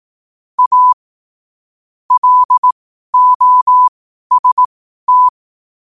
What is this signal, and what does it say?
Morse code, heard as a keyed tone, about 18 wpm, spaced as for 12 wpm, 1000 Hz. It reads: A LOST